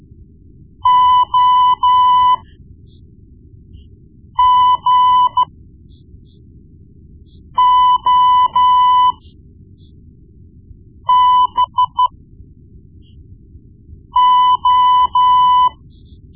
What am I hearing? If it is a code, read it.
Morse code, heard as a keyed tone, about 13 wpm, spaced as for 4 wpm, 1000 Hz. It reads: OGOBO
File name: Morse Lessons / Morse Lessons / Tape-1B